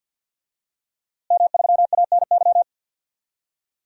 Morse code, heard as a keyed tone, about 47 wpm, 700 Hz. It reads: M3ANY
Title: DXCC-100 Call Signs - 5 Characters - Repeated 3x - 50wpm